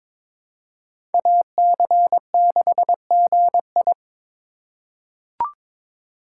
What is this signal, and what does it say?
Morse code, heard as a keyed tone, about 22 wpm, 700 Hz. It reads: AC6GI
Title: USA Call Signs - 5 Characters - 4x Character Spacing - 22wpm